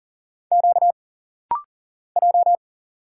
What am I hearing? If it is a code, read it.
Morse code, heard as a keyed tone, about 40 wpm, 700 Hz. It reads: Q J
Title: Single Letters - Rapid-Fire 40wpm